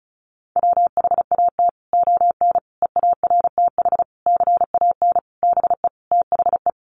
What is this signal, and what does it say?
Morse code, heard as a keyed tone, about 34 wpm, 700 Hz. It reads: WHAT ON EARTH CAN BE THE
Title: Sentences from Top 700 Words 35wpm